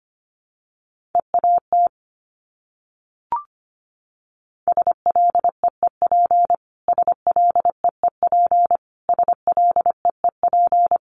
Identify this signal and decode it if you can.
Morse code, heard as a keyed tone, about 25 wpm, 700 Hz. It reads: EAT SLEEP SLEEP SLEEP
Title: Top 500 Words Repeated 25wpm